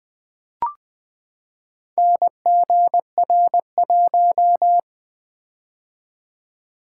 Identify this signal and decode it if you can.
Morse code, heard as a keyed tone, about 20 wpm, 700 Hz. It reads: NGR1
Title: Four Letter-Number Combinations - 20wpm